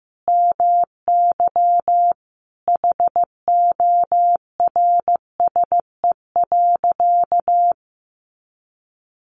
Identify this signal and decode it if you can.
Morse code, heard as a keyed tone, about 15 wpm, 700 Hz. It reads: MY HORSE.
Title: Sentences from Top 500 Words 15wpm